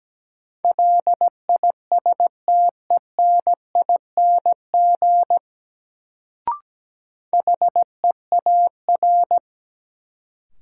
Morse code, heard as a keyed tone, about 17 wpm, 700 Hz. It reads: LISTENING HEAR
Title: Sets of 5 Words - New Word Drill - 17wpm